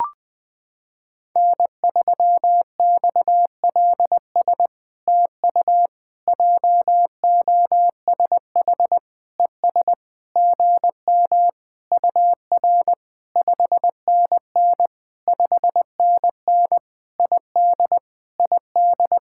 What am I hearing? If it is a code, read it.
Morse code, heard as a keyed tone, about 20 wpm, 700 Hz. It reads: N3XLS TU JOSH ES GM UR 5NN 5NN ID ID
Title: POTA QSO Practice - 20wpm